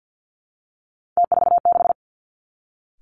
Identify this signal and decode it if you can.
Morse code, heard as a keyed tone, about 50 wpm, 700 Hz. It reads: T46
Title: Three Challenging Letter-Number Combinations 50wpm